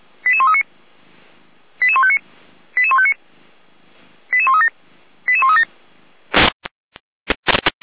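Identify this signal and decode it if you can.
Five-tone selective call (ZVEI), heard as quick runs of five tones, tones of about 70 ms. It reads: 88139 88159 88159 88137 88137
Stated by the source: ZVEI five-tone selective call received on SDR.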